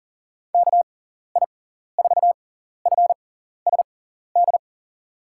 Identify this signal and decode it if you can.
Morse code, heard as a keyed tone, about 40 wpm, spaced as for 14 wpm, 700 Hz. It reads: KI4FSD
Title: USA Call Signs - 6 Characters - 6x Character Spacing - 40wpm